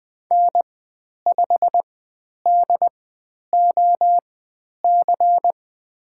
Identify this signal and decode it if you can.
Morse code, heard as a keyed tone, about 20 wpm, spaced as for 10 wpm, 700 Hz. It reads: N5DOC
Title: 102 Lesson 17g - Letter u - Call Signs 20wpm